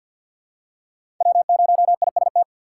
Morse code, heard as a keyed tone, about 46 wpm, 700 Hz. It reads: W0IST
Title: USA Call Signs - 5 Characters - 6x Character Spacing - 50wpm